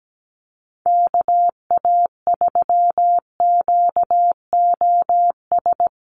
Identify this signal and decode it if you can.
Morse code, heard as a keyed tone, about 17 wpm, 700 Hz. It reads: KA3QOS
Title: USA Call Signs 17wpm